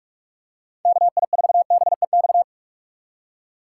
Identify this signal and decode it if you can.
Morse code, heard as a keyed tone, about 44 wpm, 700 Hz. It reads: KI4BEX
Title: USA Call Signs - 6 Characters - 8x Character Spacing - 45wpm